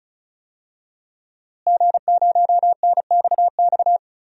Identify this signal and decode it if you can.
Morse code, heard as a keyed tone, about 36 wpm, 700 Hz. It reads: G0NXX
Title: DXCC-100 Call Signs - 5 Characters - Repeated 2x - 35wpm